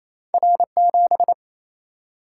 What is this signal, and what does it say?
Morse code, heard as a keyed tone, about 28 wpm, 700 Hz. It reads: R7
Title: Single-Letter Single-Number 28wpm